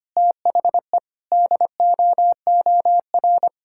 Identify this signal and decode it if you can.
Morse code, heard as a keyed tone, about 25 wpm, 700 Hz. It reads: THE DOOR
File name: practice / 25wpm / Sentences with 3-Word Phrases - 25wpm